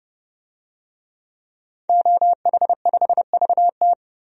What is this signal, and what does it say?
Morse code, heard as a keyed tone, about 30 wpm, 700 Hz. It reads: OH5VT
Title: DXCC-100 Call Signs - 5 Characters - 30wpm